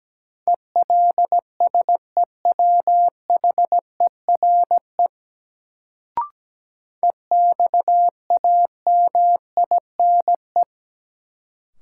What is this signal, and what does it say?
Morse code, heard as a keyed tone, about 17 wpm, 700 Hz. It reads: ELSEWHERE EXAMINE
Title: Sentences from Top 1600 Words - New Word Drill - 17wpm